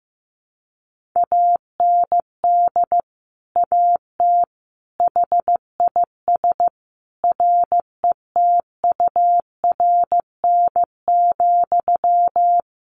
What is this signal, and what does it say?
Morse code, heard as a keyed tone, about 15 wpm, 700 Hz. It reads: AND AT HIS RETURN,
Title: Sentences from Top 1500 Words 15wpm